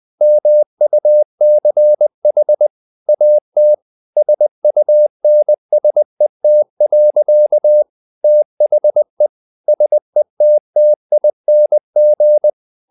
Morse code, heard as a keyed tone, about 20 wpm, 600 Hz. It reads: MUCH AT SUNSET. THE SETTING
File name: the old man and the sea part 2 20wpm